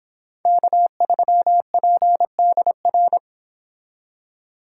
Morse code, heard as a keyed tone, about 26 wpm, 700 Hz. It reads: K3PDR